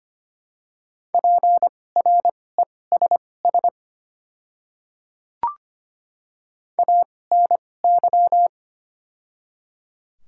Morse code, heard as a keyed tone, about 25 wpm, spaced as for 18 wpm, 700 Hz. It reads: PRESS ANY
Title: 2x Character Spacing - Top 500 Words - 25wpm